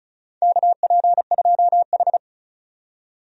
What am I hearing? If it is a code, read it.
Morse code, heard as a keyed tone, about 34 wpm, 700 Hz. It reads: KP2H
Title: DXCC-100 Call Signs - 4 Characters - 35wpm